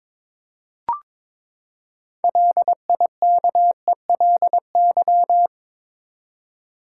Morse code, heard as a keyed tone, about 22 wpm, 700 Hz. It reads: LIKELY
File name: Build A Phrase - 2-Word Phrases - 22wpm